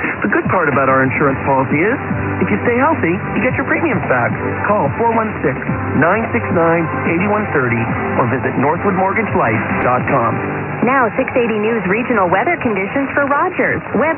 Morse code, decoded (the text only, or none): none